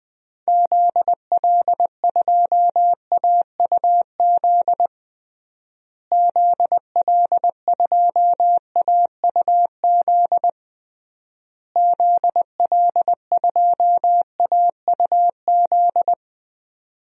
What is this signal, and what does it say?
Morse code, heard as a keyed tone, about 20 wpm, 700 Hz. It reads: ZL2AUZ ZL2AUZ ZL2AUZ
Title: DXCC-100 Call Signs - 6 Characters - Repeated 3x - 20wpm